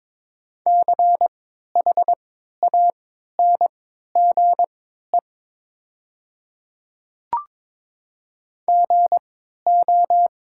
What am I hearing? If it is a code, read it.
Morse code, heard as a keyed tone, about 22 wpm, spaced as for 12 wpm, 700 Hz. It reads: CHANGE GO
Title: 3x Character Spacing - Top 500 Words - 22wpm